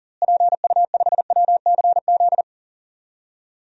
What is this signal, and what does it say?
Morse code, heard as a keyed tone, about 40 wpm, 700 Hz. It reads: PU5WCZ
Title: DXCC-100 Call Signs - 6 Characters - Repeated 3x - 40wpm